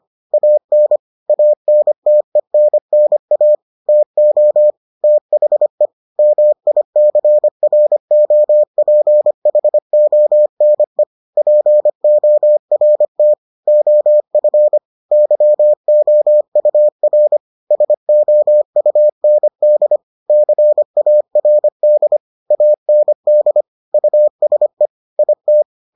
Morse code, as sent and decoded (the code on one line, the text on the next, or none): .- -. / .- -. - . -. -. .- / - --- / - .... / -- .. -.-. .-. --- .--. .... --- -. / .--. --- .-. - / --- ..-. / -.-- --- ..- .-. / ... --- ..- -. -.. / -.-. .- .-. -.. / .- -. -.. / ..- ... / .. -
AN ANTENNA TO THE MICROPHONE PORT OF YOUR SOUND CARD AND USE IT